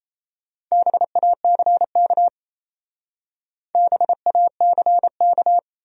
Morse code, heard as a keyed tone, about 35 wpm, 700 Hz. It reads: BACK BACK
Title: Speed-Racing Top 200 Words 28wpm